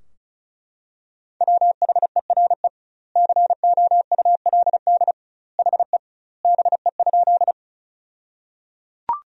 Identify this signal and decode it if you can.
Morse code, heard as a keyed tone, about 36 wpm, 700 Hz. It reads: WHERE COULD HE BE?
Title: 2x Word-Spacing - Sentences from Top 200 Words - 35wpm